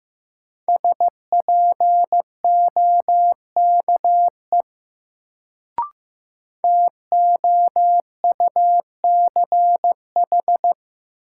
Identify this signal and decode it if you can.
Morse code, heard as a keyed tone, about 15 wpm, 700 Hz. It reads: SPOKE TOUCH
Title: Sentences from Top 1200 Words - New Word Drill Alt - 15wpm